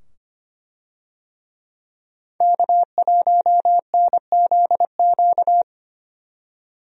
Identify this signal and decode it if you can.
Morse code, heard as a keyed tone, about 25 wpm, 700 Hz. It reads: K1NZQ
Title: USA Call Signs - 5 Characters - 25wpm